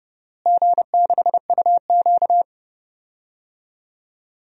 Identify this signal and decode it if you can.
Morse code, heard as a keyed tone, about 30 wpm, 700 Hz. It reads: G6UQ
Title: DXCC-100 Call Signs - 4 Characters - Repeated 3x - 30wpm